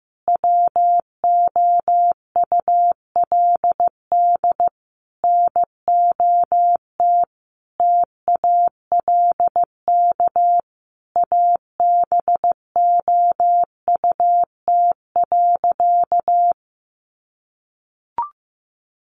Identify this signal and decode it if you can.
Morse code, heard as a keyed tone, about 15 wpm, 700 Hz. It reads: WOULD NOT TALK ABOUT.
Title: Sentences from Top 800 Words 15wpm